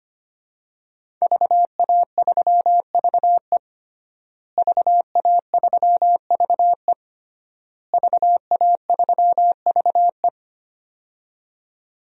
Morse code, heard as a keyed tone, about 25 wpm, 700 Hz. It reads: VA3VE VA3VE VA3VE
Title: DXCC-100 Call Signs - 5 Characters - Repeated 3x - 25wpm